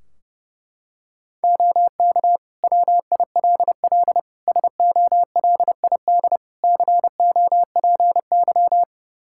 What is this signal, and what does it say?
Morse code, heard as a keyed tone, about 32 wpm, 700 Hz. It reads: OK WILL SOLID COPY
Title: Practice QSO - Exchange 3 - More Info - 30wpm